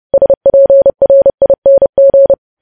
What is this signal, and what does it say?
Morse code, heard as a keyed tone, about 30 wpm, 550 Hz. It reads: SPRING
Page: news headlines in morse code at 30 wpm